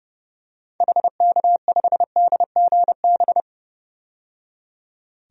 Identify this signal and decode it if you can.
Morse code, heard as a keyed tone, about 32 wpm, 700 Hz. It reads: HK5DGB
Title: DXCC-100 Call Signs - 6 Characters - 30wpm